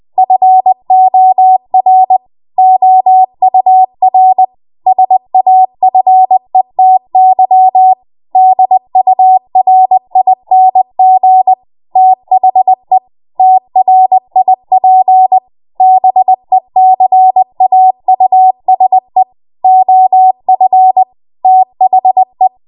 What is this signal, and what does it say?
Morse code, heard as a keyed tone, about 20 wpm, 750 Hz. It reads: FOR OUR SAFETY DURING THE TRIP BECAUSE OF THE
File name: Morse / 20WPM